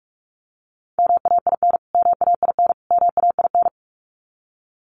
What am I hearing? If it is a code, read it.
Morse code, heard as a keyed tone, about 44 wpm, 700 Hz. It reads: MAIN MAIN MAIN
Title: Top 300 Words Repeated 45wpm